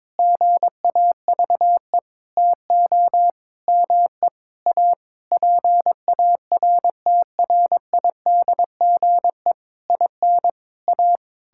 Morse code, heard as a keyed tone, about 22 wpm, 700 Hz. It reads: GAVE TO ME A PARTRIDGE IN A
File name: Sentences with 9-Word Phrases - 22wpm